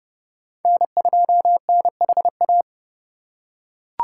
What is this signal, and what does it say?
Morse code, heard as a keyed tone, about 31 wpm, 700 Hz. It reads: N2NHA
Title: USA Call Signs - 3x Character Spacing - 30wpm